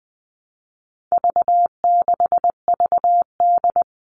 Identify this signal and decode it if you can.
Morse code, heard as a keyed tone, about 20 wpm, 700 Hz. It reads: V6VD
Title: All Cut Numbers - 4-Digit 20wpm